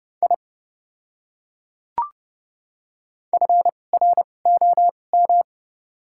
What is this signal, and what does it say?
Morse code, heard as a keyed tone, about 30 wpm, spaced as for 22 wpm, 700 Hz. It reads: I FROM
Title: 2x Character Spacing - Top 300 Words - 30wpm